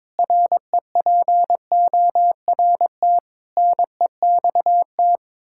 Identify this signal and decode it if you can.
Morse code, heard as a keyed tone, about 22 wpm, 700 Hz. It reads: REPORT NEXT
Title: Sentences from Top 1000 Words - Review Part 6 - 22wpm